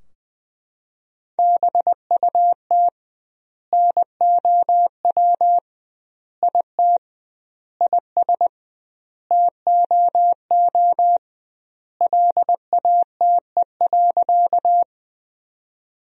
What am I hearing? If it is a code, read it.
Morse code, heard as a keyed tone, about 21 wpm, 700 Hz. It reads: BUT NOW IT IS TOO LATE.
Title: 2x Word-Spacing - Sentences from Top 300 Words - 20wpm